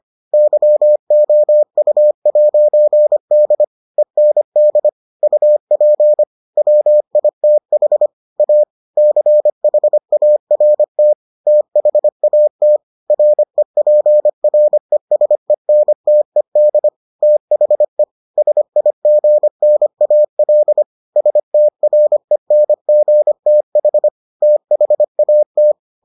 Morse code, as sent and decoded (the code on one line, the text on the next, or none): -.-- --- ..- .----. -.. / . -. -.. / ..- .--. / .-- .. - .... / .- / -.-. .... .- .-. - / - .... .- - / .-. .--. .-. ... -. - . -.. / - .... / ... .. --. -. .- .-.. / ... - .-. -. --. - .... / - .... .- -
YOU'D END UP WITH A CHART THAT REPRESENTED THE SIGNAL STRENGTH THAT